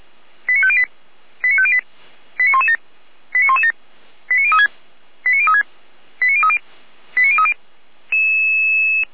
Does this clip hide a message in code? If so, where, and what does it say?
Yes: five-tone selective call (ZVEI) from 0.5 to 7.5 s, heard as quick runs of five tones, tones of about 70 ms. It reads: 89508 89508 89117 89117 89036 89036 89030 89030